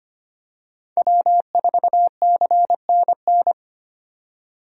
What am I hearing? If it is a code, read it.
Morse code, heard as a keyed tone, about 25 wpm, 700 Hz. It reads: W4CNN
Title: USA Call Signs - 5 Characters - 25wpm